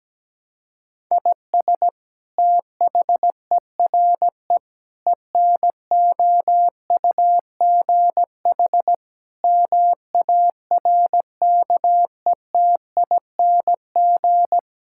Morse code, heard as a keyed tone, about 17 wpm, 700 Hz. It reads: IS THERE ENOUGH MARKETING
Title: Sentences from Top 1000 Words - Review Part 5 - 17wpm